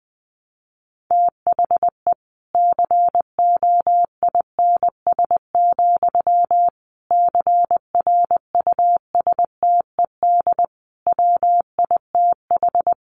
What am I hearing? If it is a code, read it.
Morse code, heard as a keyed tone, about 20 wpm, 700 Hz. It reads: THE COINS, CRUSTED WITH
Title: A Sound Of Thunder 20wpm